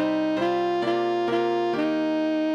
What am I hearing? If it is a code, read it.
There is no code.